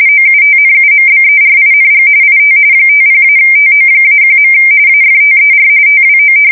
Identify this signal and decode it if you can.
RTTY, about 45.45 baud. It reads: OUTNUMBERING IT 50-TO-1. AFTER TWO
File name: fsk170 / usnavy2